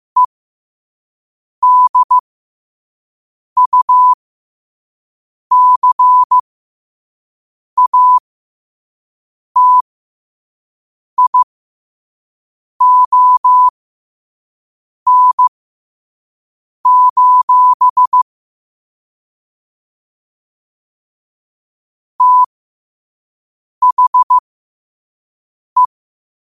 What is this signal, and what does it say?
Morse code, heard as a keyed tone, about 15 wpm, spaced as for 5 wpm, 1000 Hz. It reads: EDUCATION: THE
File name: qotdfiles / 5wpm